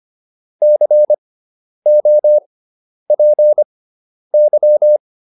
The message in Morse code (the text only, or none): COPY